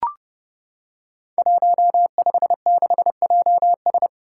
Morse code, heard as a keyed tone, about 30 wpm, 700 Hz. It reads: 156JS
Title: Five Challenging Letter-Number Combinations 30wpm